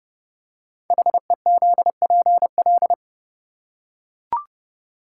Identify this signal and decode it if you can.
Morse code, heard as a keyed tone, about 30 wpm, 700 Hz. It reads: HEZPL
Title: Code Groups - 5 Letters - 30wpm